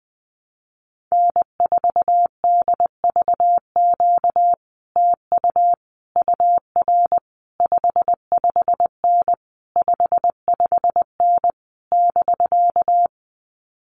Morse code, heard as a keyed tone, about 20 wpm, 700 Hz. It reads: N4DVQ TU UR 55N 55N <BK>
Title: Common QSO Phrases Encore 20wpm